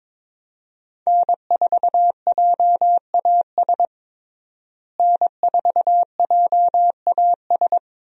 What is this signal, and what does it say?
Morse code, heard as a keyed tone, about 22 wpm, 700 Hz. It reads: N4JAS N4JAS
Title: USA Call Signs - 5 Characters - Repeated 3x - 22wpm